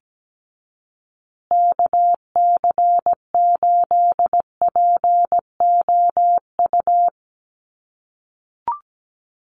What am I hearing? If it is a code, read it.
Morse code, heard as a keyed tone, about 17 wpm, 700 Hz. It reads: KC8POU
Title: USA Call Signs Repeated 3x 17wpm